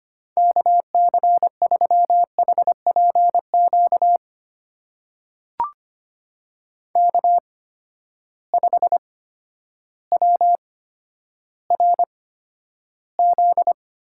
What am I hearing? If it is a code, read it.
Morse code, heard as a keyed tone, about 25 wpm, spaced as for 11 wpm, 700 Hz. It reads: KC3HPQ K5WRZ
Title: USA Call Signs - 8x Character Spacing - 25wpm